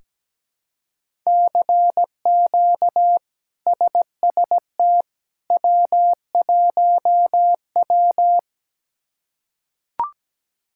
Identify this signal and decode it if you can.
Morse code, heard as a keyed tone, about 17 wpm, 700 Hz. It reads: CQ SST W1W
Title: K1USN SST QSO Practice - 17wpm